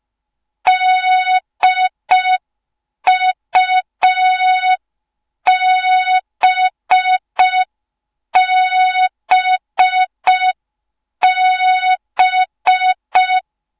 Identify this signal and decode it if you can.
Morse code, heard as a keyed tone, about 5 wpm, 3700 Hz. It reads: DUBBB
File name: Morze-HA2MN / hang / 06BDU HA2MN